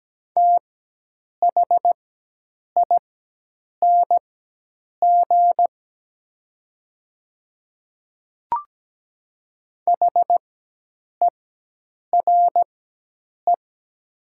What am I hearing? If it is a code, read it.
Morse code, heard as a keyed tone, about 17 wpm, spaced as for 8 wpm, 700 Hz. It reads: THING HERE
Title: 4x Character Spacing - Top 500 Words - 17wpm